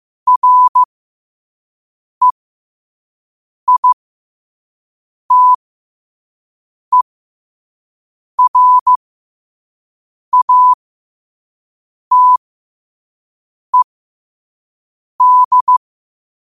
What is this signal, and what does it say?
Morse code, heard as a keyed tone, about 15 wpm, spaced as for 5 wpm, 1000 Hz. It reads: REITERATED
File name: qotdfiles / 5wpm